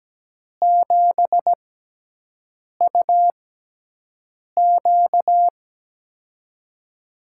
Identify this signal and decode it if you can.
Morse code, heard as a keyed tone, about 17 wpm, spaced as for 6 wpm, 700 Hz. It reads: 7UQ
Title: USA Call Signs - 6x Character Spacing - 17wpm